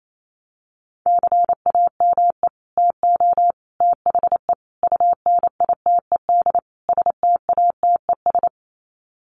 Morse code, heard as a keyed tone, about 28 wpm, 700 Hz. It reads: CAME TO THE UNITED STATES
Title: Sets of 5 Words Encore2 28wpm